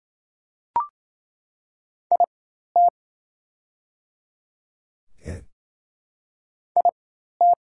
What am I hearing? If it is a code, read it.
Morse code, heard as a keyed tone, about 27 wpm, spaced as for 13 wpm, 700 Hz. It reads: IT IT
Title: 4x Character Spacing - Top 100 Words - 28wpm